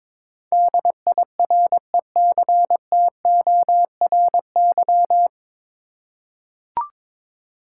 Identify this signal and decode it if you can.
Morse code, heard as a keyed tone, about 22 wpm, 700 Hz. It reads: DIRECTORY
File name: 9 Letter Words - First 3 Letters Given - 22wpm